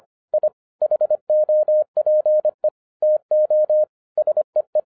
Morse code, heard as a keyed tone, about 25 wpm, 610 Hz. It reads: I HOPE TO SEE